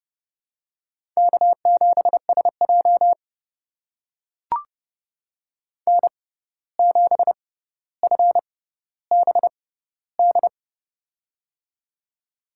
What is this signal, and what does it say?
Morse code, heard as a keyed tone, about 32 wpm, spaced as for 15 wpm, 700 Hz. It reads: K7SJ N7FBD
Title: USA Call Signs - 6x Character Spacing - 30wpm